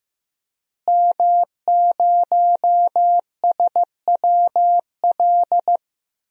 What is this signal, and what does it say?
Morse code, heard as a keyed tone, about 15 wpm, 700 Hz. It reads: M0SWL
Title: DXCC-100 Call Signs - 5 Characters - Repeated 3x - 15wpm